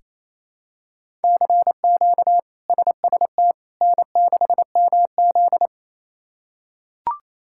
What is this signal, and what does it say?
Morse code, heard as a keyed tone, about 28 wpm, 700 Hz. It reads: CQ SST N6MZ
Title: K1USN SST QSO Practice - 28wpm